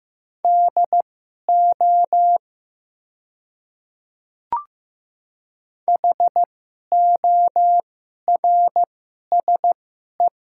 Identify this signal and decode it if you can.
Morse code, heard as a keyed tone, about 15 wpm, spaced as for 11 wpm, 700 Hz. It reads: DO HORSE
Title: 2x Character Spacing - Top 300 Words - 15wpm